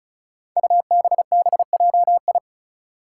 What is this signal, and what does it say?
Morse code, heard as a keyed tone, about 34 wpm, 700 Hz. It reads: UBBJI
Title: Five Challenging Letter-Number Combinations 35wpm